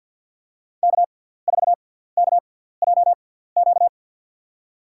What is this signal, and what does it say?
Morse code, heard as a keyed tone, about 49 wpm, spaced as for 17 wpm, 700 Hz. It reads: K4KJQ
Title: USA Call Signs - 5 Characters - 6x Character Spacing - 50wpm